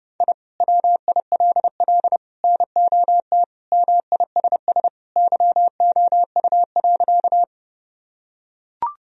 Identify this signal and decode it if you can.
Morse code, heard as a keyed tone, about 32 wpm, 700 Hz. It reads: I WILL NOT MISS YOU.
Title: Sentences from Top 1000 Words - Review Part 3 - 30wpm